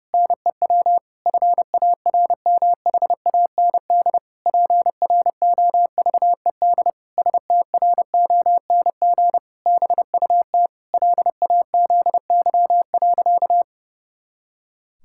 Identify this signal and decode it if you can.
Morse code, heard as a keyed tone, about 30 wpm, 700 Hz. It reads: NEW FARMHAND PROVED STRONG BUT LAZY.
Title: Pangram Challenge v2 - 30wpm